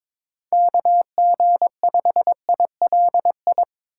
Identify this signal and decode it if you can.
Morse code, heard as a keyed tone, about 24 wpm, 700 Hz. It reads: KG5ILI
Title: Speed Racing - USA 2x3 Call Signs 22wpm